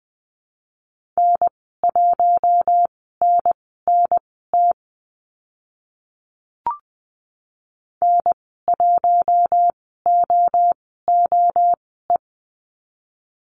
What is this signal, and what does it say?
Morse code, heard as a keyed tone, about 20 wpm, spaced as for 15 wpm, 700 Hz. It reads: N1NNT N1OOE
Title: 042 Lesson 8g - Number 1 - Call Signs 20wpm